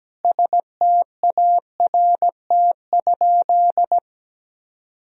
Morse code, heard as a keyed tone, about 17 wpm, 700 Hz. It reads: START?
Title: Sentences from Top 1000 Words - Review Part 1 - 17wpm